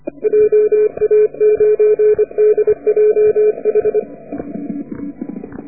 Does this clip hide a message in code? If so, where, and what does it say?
Yes: Morse code from 0.2 to 4.0 s, heard as a keyed tone, about 30 wpm, 440 Hz. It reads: JA9DJH